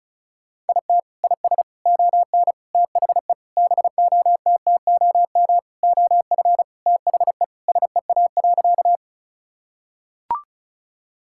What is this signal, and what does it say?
Morse code, heard as a keyed tone, about 36 wpm, 700 Hz. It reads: IT IS ON THE BOTTOM OF THE SEA.
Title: Sentences from Top 1000 Words - Review Part 6 - 35wpm